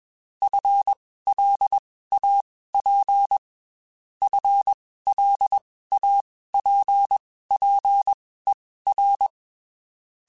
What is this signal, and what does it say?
Morse code, heard as a keyed tone, about 21 wpm, spaced as for 15 wpm, 800 Hz. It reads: FLAP FLAPPER